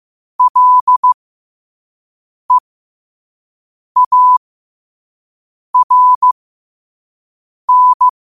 Morse code, heard as a keyed tone, about 15 wpm, spaced as for 5 wpm, 1000 Hz. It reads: LEARN